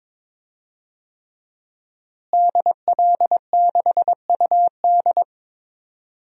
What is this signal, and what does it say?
Morse code, heard as a keyed tone, about 22 wpm, 700 Hz. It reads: DL6UD
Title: DXCC-100 Call Signs - 5 Characters - Repeated 3x - 22wpm